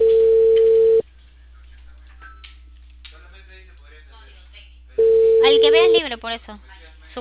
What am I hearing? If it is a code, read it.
Ringback tone (China, 450 Hz), with speech.